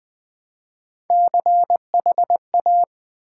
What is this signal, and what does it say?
Morse code, heard as a keyed tone, about 20 wpm, 700 Hz. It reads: CHA
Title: Build A Word - 1 Letter At A Time - 5-Letter Words - 20wpm